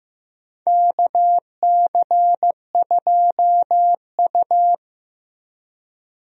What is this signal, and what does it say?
Morse code, heard as a keyed tone, about 15 wpm, 700 Hz. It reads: KC2U
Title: USA Call Signs - 4 Characters - Repeated 2x - 15wpm